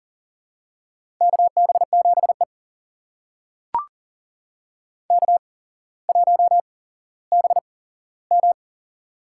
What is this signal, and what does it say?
Morse code, heard as a keyed tone, about 40 wpm, spaced as for 17 wpm, 700 Hz. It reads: KB7E K1BM
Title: USA Call Signs - 8x Character Spacing - 40wpm